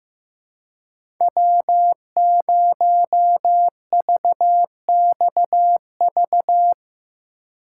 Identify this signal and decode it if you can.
Morse code, heard as a keyed tone, about 15 wpm, 700 Hz. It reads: W0VXV
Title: USA Call Signs - 5 Characters - 3x Character Spacing - 15wpm